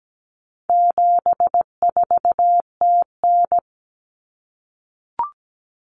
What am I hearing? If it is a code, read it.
Morse code, heard as a keyed tone, about 17 wpm, 700 Hz. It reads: 74TN
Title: Common Cut Numbers - 4-Digit 17wpm